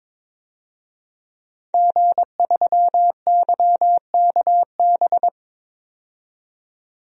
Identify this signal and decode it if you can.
Morse code, heard as a keyed tone, about 21 wpm, 700 Hz. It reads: G3YKB
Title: DXCC-100 Call Signs - 5 Characters - 22wpm